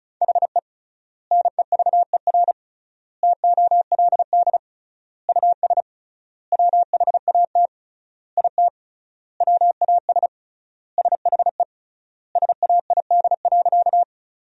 Morse code, heard as a keyed tone, about 34 wpm, 700 Hz. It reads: HE NEVER TOLD US WHAT IT WAS SHE SAID.
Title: 3x Word-Spacing - Sentences from Top 400 Words - 35wpm